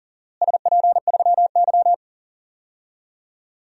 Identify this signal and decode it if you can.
Morse code, heard as a keyed tone, about 40 wpm, 700 Hz. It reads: SP3Y
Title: DXCC-100 Call Signs - 4 Characters - Repeated 3x - 40wpm